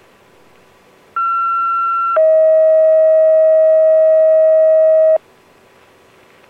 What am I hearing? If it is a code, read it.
Two-tone page, heard as a one-second tone, then a three-second tone, about 1360 then 640 Hz. It is heard from 1.2 to 5.2 s.